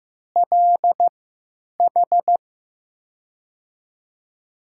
Morse code, heard as a keyed tone, about 15 wpm, spaced as for 9 wpm, 700 Hz. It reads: LH
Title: USA Call Signs - 6 Characters - 3x Character Spacing - 15wpm